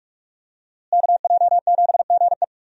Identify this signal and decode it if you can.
Morse code, heard as a keyed tone, about 46 wpm, 700 Hz. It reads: KJ7GE
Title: USA Call Signs - 5 Characters - Repeated 3x - 45wpm